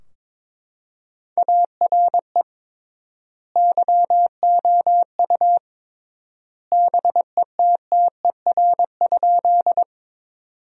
Morse code, heard as a keyed tone, about 22 wpm, 700 Hz. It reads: ARE YOU BETTER?
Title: 3x Word-Spacing - Sentences from Top 400 Words - 22wpm